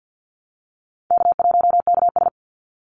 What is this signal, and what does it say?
Morse code, heard as a keyed tone, about 50 wpm, 700 Hz. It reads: K1KS
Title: Easy Call Signs - 50wpm